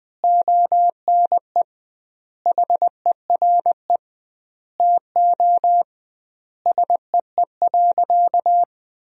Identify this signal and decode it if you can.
Morse code, heard as a keyed tone, about 21 wpm, 700 Hz. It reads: ONE HERE TO SEE.
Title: 2x Word-Spacing - Sentences from Top 200 Words - 20wpm